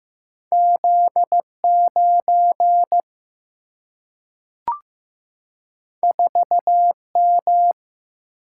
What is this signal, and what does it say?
Morse code, heard as a keyed tone, about 15 wpm, 700 Hz. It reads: Z 9 4 M
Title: Single-Letter Single-Number 15wpm